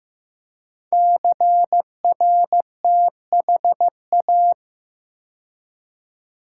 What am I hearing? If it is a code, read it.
Morse code, heard as a keyed tone, about 15 wpm, 700 Hz. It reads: CRTHA
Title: Code Groups - 5 Letters - 15wpm